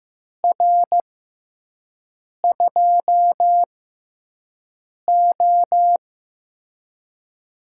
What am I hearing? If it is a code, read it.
Morse code, heard as a keyed tone, about 15 wpm, spaced as for 5 wpm, 700 Hz. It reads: R2O
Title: USA Call Signs - 6x Character Spacing - 15wpm